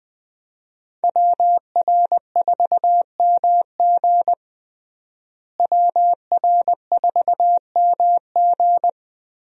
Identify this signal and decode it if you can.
Morse code, heard as a keyed tone, about 20 wpm, 700 Hz. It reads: WR4MG WR4MG
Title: USA Call Signs - 5 Characters - Repeated 3x - 20wpm